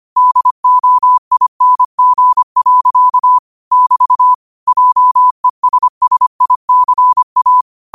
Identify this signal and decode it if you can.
Morse code, heard as a keyed tone, about 25 wpm, 1000 Hz. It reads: DOING. <BT> JESSICA